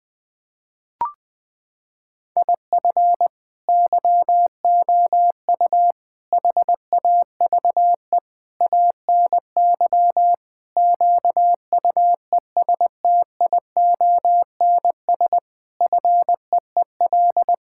Morse code, heard as a keyed tone, about 20 wpm, 700 Hz. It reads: IF YOU HAVE ANY QUESTIONS FEEL
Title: Sentences with 8-Word Phrases - 20wpm